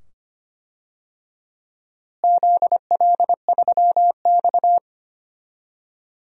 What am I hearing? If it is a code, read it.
Morse code, heard as a keyed tone, about 25 wpm, 700 Hz. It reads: ZL3X
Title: DXCC-100 Call Signs - 4 Characters - Repeated 2x - 25wpm